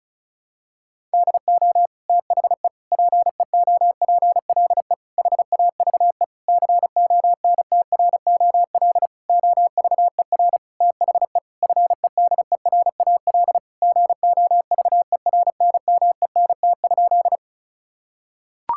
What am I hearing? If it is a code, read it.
Morse code, heard as a keyed tone, about 36 wpm, 700 Hz. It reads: DO THE PEOPLE HAVE CONTROL OVER THE FEDERAL GOVERNMENT?
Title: Sentences from Top 1000 Words - Review Part 5 - 35wpm